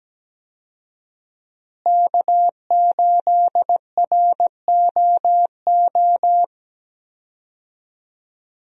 Morse code, heard as a keyed tone, about 17 wpm, 700 Hz. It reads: K8ROO
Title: USA Call Signs - 5 Characters - Repeated 3x - 17wpm